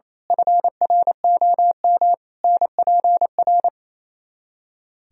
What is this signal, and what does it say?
Morse code, heard as a keyed tone, about 27 wpm, 700 Hz. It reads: FROM NPR
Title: Speed-Racing - Sets of 5 Words 28wpm